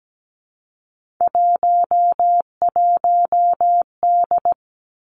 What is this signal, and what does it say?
Morse code, heard as a keyed tone, about 17 wpm, 700 Hz. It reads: 11D
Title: Three Challenging Letter-Number Combinations 17wpm